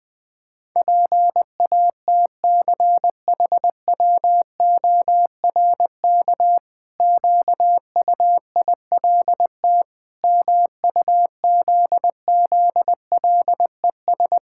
Morse code, heard as a keyed tone, about 20 wpm, 700 Hz. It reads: PATCHWORK QUILT MUZZLES
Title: Pangram Challenge v2 - 20wpm